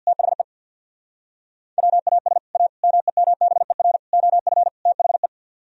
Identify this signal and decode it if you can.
Morse code, heard as a keyed tone, about 65 wpm, 700 Hz. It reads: THE WAS A MEMBER OF THE